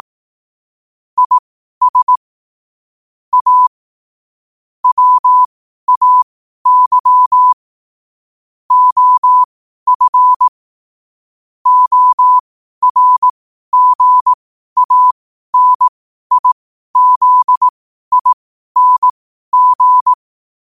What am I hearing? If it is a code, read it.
Morse code, heard as a keyed tone, about 18 wpm, spaced as for 12 wpm, 1000 Hz. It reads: IS A WAY OF ORGANIZING
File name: qotdfiles / 12wpm